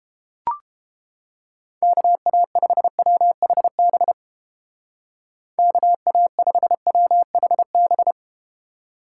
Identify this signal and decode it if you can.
Morse code, heard as a keyed tone, about 32 wpm, 700 Hz. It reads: KA5WHB KA5WHB